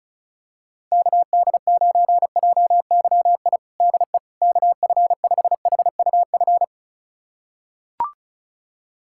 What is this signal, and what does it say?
Morse code, heard as a keyed tone, about 36 wpm, 700 Hz. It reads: KD9JYI DE KF5HUF
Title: USA Callsign DE Callsign 35wpm